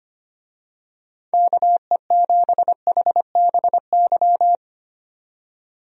Morse code, heard as a keyed tone, about 25 wpm, 700 Hz. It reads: KE7HBY
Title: USA Call Signs - 6 Characters - 25wpm